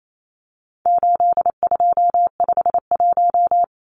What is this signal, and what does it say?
Morse code, heard as a keyed tone, about 28 wpm, 700 Hz. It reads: 8251
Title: Numbers - 4-Digit 28wpm